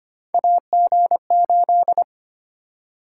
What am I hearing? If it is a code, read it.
Morse code, heard as a keyed tone, about 25 wpm, 700 Hz. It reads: AG8
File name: Three Letter-Number Combinations - 25wpm